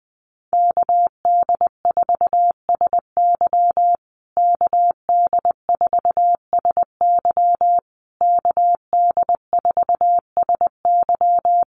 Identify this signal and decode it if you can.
Morse code, heard as a keyed tone, about 20 wpm, 700 Hz. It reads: KD4SY KD4SY KD4SY
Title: USA Call Signs Repeated 3x 20wpm